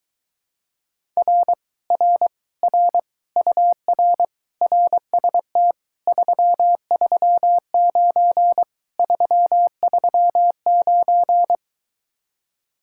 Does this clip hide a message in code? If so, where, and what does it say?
Yes: Morse code from 1.2 to 11.6 s, heard as a keyed tone, about 23 wpm, 700 Hz. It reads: R R R UR RST 339 339